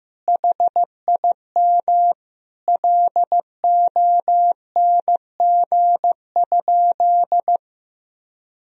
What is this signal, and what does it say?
Morse code, heard as a keyed tone, about 16 wpm, 700 Hz. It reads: HIM LONG?
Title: Sentences from Top 1000 Words - Review Part 3 - 15wpm